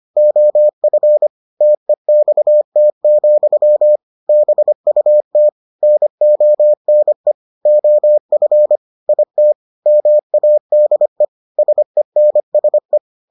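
Morse code, heard as a keyed tone, about 25 wpm, 600 Hz. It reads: OF TEXT, BUT NONE OF IT MADE SENSE
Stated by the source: Morse edition of the Foundations of Amateur Radio episodes.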